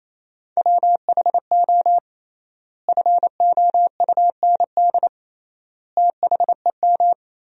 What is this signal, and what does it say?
Morse code, heard as a keyed tone, about 28 wpm, 700 Hz. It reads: WHO FOUND THEM
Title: 3x Word-Spacing - Sentences from Top 300 Words - 28wpm